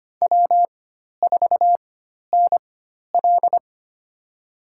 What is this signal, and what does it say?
Morse code, heard as a keyed tone, about 25 wpm, spaced as for 12 wpm, 690 Hz. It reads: W4NL
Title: USA Call Signs - 4 Characters - 4x Character Spacing - 25wpm